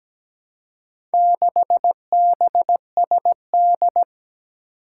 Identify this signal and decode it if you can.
Morse code, heard as a keyed tone, about 17 wpm, 700 Hz. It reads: 6BSD
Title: Four Challenging Letter-Number Combinations 17wpm